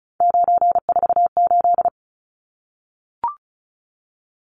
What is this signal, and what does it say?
Morse code, heard as a keyed tone, about 34 wpm, 700 Hz. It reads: 948
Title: Numbers - 3-Digit 35wpm